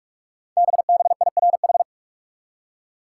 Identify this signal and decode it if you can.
Morse code, heard as a keyed tone, about 44 wpm, 700 Hz. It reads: BBIRH
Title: Code Groups - 5 Letters - 45wpm